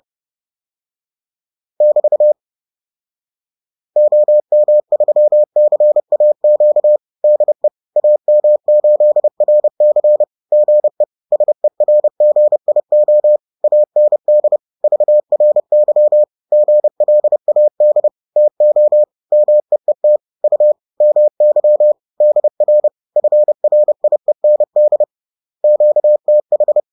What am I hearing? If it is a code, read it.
Morse code, heard as a keyed tone, about 30 wpm, 600 Hz. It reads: <BT> OM3CAQ DE AM8RC GE SERGIO AND VRY GLAD TO MEET U MY DR FRIEND QTH